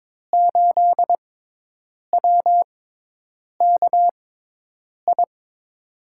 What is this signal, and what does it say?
Morse code, heard as a keyed tone, about 22 wpm, spaced as for 8 wpm, 700 Hz. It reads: 8WKI
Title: USA Call Signs - 6 Characters - 6x Character Spacing - 22wpm